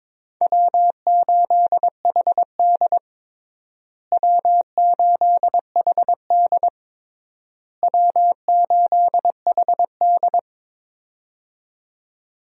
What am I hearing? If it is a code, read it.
Morse code, heard as a keyed tone, about 22 wpm, 700 Hz. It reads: W8HD W8HD W8HD
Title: USA Call Signs - 4 Characters - Repeated 3x - 22wpm